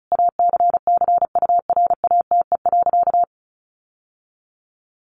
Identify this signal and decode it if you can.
Morse code, heard as a keyed tone, about 34 wpm, 700 Hz. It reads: ACCURATE.
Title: Sentences from Top 1200 Words 35wpm